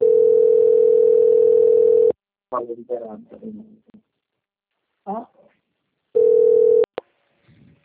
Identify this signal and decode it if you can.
Ringback tone (North America), 440 + 480 Hz, with speech.
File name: out- - - - - .39300